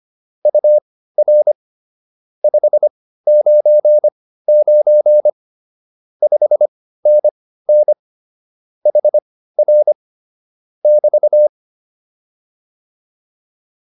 Morse code, heard as a keyed tone, about 25 wpm, spaced as for 15 wpm, 600 Hz. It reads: UR 599 5NN HR <BT>